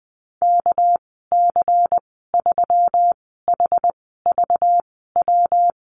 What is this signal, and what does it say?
Morse code, heard as a keyed tone, about 20 wpm, spaced as for 14 wpm, 700 Hz. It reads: KC3HVW
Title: 245 Lesson 39g - Letter z - Call Signs 20wpm